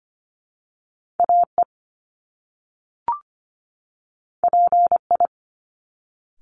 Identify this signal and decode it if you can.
Morse code, heard as a keyed tone, about 25 wpm, 700 Hz. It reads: AE PI
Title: Top Two Letter Combinations in Words 25wpm